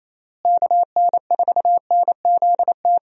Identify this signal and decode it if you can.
Morse code, heard as a keyed tone, about 27 wpm, 700 Hz. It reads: KN4NZT
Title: USA Call Signs - 6 Characters - Repeated 3x - 28wpm